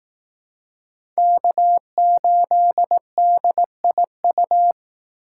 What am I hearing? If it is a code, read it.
Morse code, heard as a keyed tone, about 18 wpm, 700 Hz. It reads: K8DIU
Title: Speed Racing - USA 1x3 Call Signs 15wpm